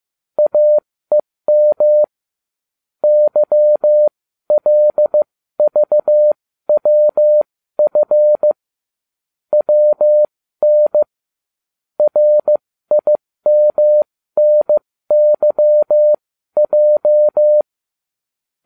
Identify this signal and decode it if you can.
Morse code, heard as a keyed tone, about 15 wpm, spaced as for 13 wpm, 600 Hz. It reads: AEM YLVWF WN RIMNYJ
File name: cw / ubacw20 3